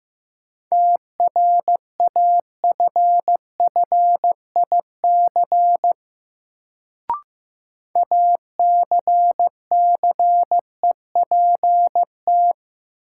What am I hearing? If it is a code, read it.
Morse code, heard as a keyed tone, about 15 wpm, 700 Hz. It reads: TRAFFIC ACCEPT
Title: Sentences from Top 1300 Words - New Word Drill - 15wpm